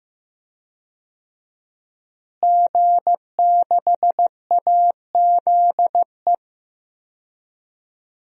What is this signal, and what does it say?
Morse code, heard as a keyed tone, about 15 wpm, 700 Hz. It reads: G6AZE
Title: DXCC-100 Call Signs - 5 Characters - Repeated 2x - 15wpm